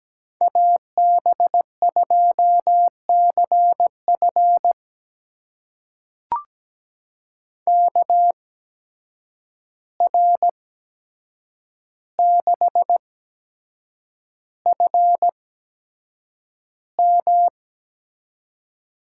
Morse code, heard as a keyed tone, about 17 wpm, spaced as for 7 wpm, 700 Hz. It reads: AB2CF KR6FM
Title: USA Call Signs - 5 Characters - 8x Character Spacing - 17wpm